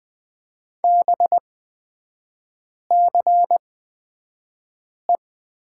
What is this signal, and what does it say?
Morse code, heard as a keyed tone, about 20 wpm, spaced as for 5 wpm, 700 Hz. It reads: BCE
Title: 224 Lesson 36e - BT - Three Characters 20wpm